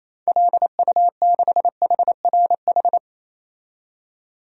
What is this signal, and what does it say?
Morse code, heard as a keyed tone, about 27 wpm, 700 Hz. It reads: LU6HRH
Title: DXCC-100 Call Signs - 6 Characters - 28wpm